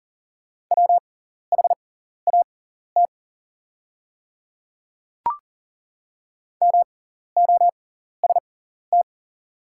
Morse code, heard as a keyed tone, about 40 wpm, spaced as for 14 wpm, 700 Hz. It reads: WHAT MOST
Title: 6x Character Spacing - Top 200 Words - 40wpm